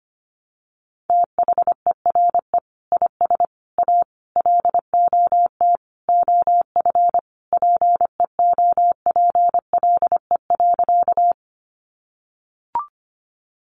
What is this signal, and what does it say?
Morse code, heard as a keyed tone, about 25 wpm, 700 Hz. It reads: THERE IS A LOT OF PEOPLE.
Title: Sentences from Top 500 Words 25wpm